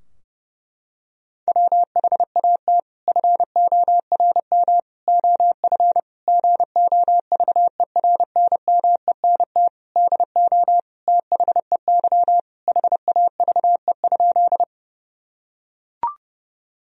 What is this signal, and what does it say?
Morse code, heard as a keyed tone, about 30 wpm, 700 Hz. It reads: WHAT FORM OF GOVERNMENT DO THEY HAVE?
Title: Sentences from Top 1000 Words - Review Part 3 - 30wpm